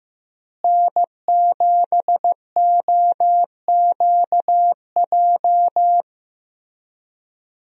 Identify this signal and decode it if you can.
Morse code, heard as a keyed tone, about 15 wpm, 700 Hz. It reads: N7OQJ
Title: USA Call Signs - 5 Characters - Repeated 3x - 15wpm